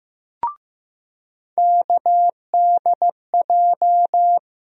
Morse code, heard as a keyed tone, about 15 wpm, 700 Hz. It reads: KDJ
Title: USA Call Sign Postfixes - 3 Letters 15wpm